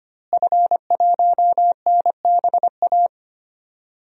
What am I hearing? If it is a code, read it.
Morse code, heard as a keyed tone, about 25 wpm, 700 Hz. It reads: F1NBA
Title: DXCC-100 Call Signs - 5 Characters - Repeated 2x - 25wpm